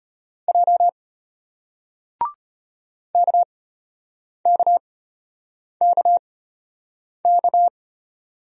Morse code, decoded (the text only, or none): J KKKK